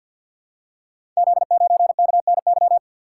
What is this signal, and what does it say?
Morse code, heard as a keyed tone, about 48 wpm, 700 Hz. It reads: Z9KNY
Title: Code Groups - 5 Letter:Numbers - 50wpm